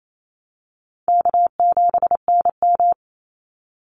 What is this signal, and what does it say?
Morse code, heard as a keyed tone, about 28 wpm, 700 Hz. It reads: K7NM
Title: Call Signs Repeated 2x - 28wpm